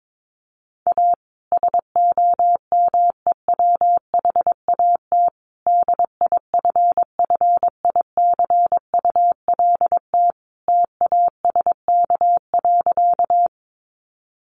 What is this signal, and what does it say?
Morse code, heard as a keyed tone, about 22 wpm, 700 Hz. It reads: A SOMEWHAT DIFFICULT TASK.
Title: Sentences from Top 1300 Words 22wpm